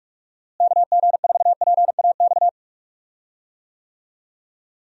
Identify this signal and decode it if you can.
Morse code, heard as a keyed tone, about 45 wpm, 700 Hz. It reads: KG4PAX
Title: USA Call Signs - 6 Characters - Repeated 2x - 45wpm